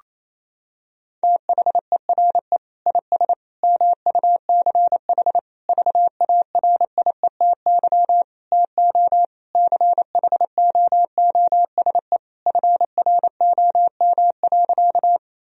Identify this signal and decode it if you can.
Morse code, heard as a keyed tone, about 28 wpm, 700 Hz. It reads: THERE IS MUCH VARIETY TO CHOOSE FROM.
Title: Sentences from Top 1000 Words - Review Part 4 - 28wpm